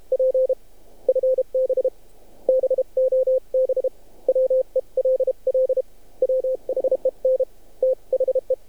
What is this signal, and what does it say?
Morse code, heard as a keyed tone, about 30 wpm, 520 Hz. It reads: P FB BOB WELL WHEN THE
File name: archived / CRSnet